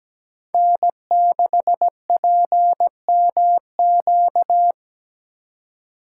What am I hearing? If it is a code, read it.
Morse code, heard as a keyed tone, about 17 wpm, 700 Hz. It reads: N6PMQ
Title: USA Call Signs - 5 Characters - 2x Character Spacing - 17wpm